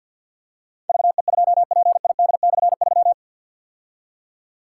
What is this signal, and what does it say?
Morse code, heard as a keyed tone, about 47 wpm, 690 Hz. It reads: VE2PID/3
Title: K1USN SST QSO Practice - 50wpm